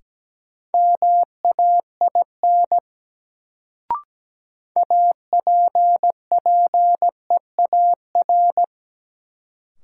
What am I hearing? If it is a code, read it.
Morse code, heard as a keyed tone, about 17 wpm, 700 Hz. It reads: MAIN APPEAR
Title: Sentences from Top 1000 Words - Review Part 2 - New Word Drill - 17wpm